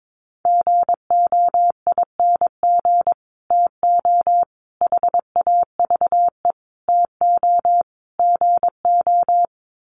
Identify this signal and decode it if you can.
Morse code, heard as a keyed tone, about 22 wpm, 700 Hz. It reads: GOING TO HAVE TO GO
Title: Sets of 5 Words Encore 22wpm